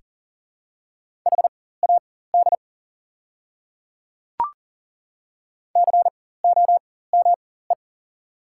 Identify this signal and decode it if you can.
Morse code, heard as a keyed tone, about 40 wpm, spaced as for 19 wpm, 690 Hz. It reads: HAD COME
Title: 4x Character Spacing - Top 100 Words - 40wpm